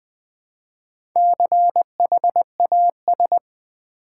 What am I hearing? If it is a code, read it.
Morse code, heard as a keyed tone, about 20 wpm, 700 Hz. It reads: CHAS
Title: Build A Word - 1 Letter At A Time - 5-Letter Words - 20wpm